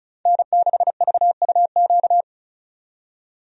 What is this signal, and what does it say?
Morse code, heard as a keyed tone, about 36 wpm, 700 Hz. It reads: N6VUQ
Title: USA Call Signs - 5 Characters - 35wpm